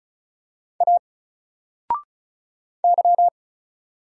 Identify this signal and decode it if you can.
Morse code, heard as a keyed tone, about 36 wpm, 700 Hz. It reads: A Y
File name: Single-Letter-Number-35wpm